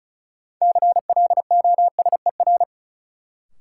Morse code, heard as a keyed tone, about 34 wpm, 700 Hz. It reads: CLOSER
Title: Sets of 5 Words Encore - New Word Drill - 35wpm